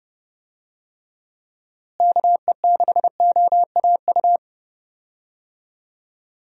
Morse code, heard as a keyed tone, about 30 wpm, 700 Hz. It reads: KE6OAU
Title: USA Call Signs - 6 Characters - Repeated 3x - 30wpm